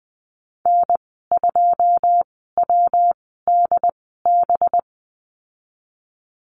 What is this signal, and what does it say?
Morse code, heard as a keyed tone, about 20 wpm, spaced as for 15 wpm, 700 Hz. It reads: N2WDB
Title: 193 Lesson 31g - Letter v - Call Signs 20wpm